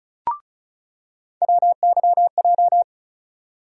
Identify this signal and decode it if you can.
Morse code, heard as a keyed tone, about 36 wpm, 700 Hz. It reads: WYJ
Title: USA Call Sign Postfixes - 3 Letters 35wpm